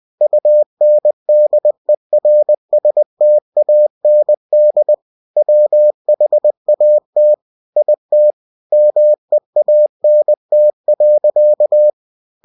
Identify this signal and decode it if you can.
Morse code, heard as a keyed tone, about 20 wpm, 600 Hz. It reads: UNDERSTAND WHAT IT MEANT.
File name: cw-books / winnie-the-pooh / winnie-20wpm-page-0071